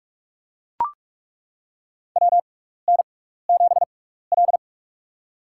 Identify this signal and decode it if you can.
Morse code, heard as a keyed tone, about 41 wpm, spaced as for 15 wpm, 700 Hz. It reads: WN7L
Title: USA Call Signs - 6x Character Spacing - 45wpm